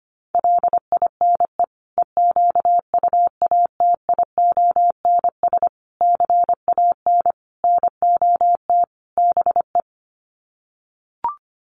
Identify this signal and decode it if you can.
Morse code, heard as a keyed tone, about 25 wpm, 700 Hz. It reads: LINE EQUATIONS CAN NOT BE
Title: Sets of 5 Words 25wpm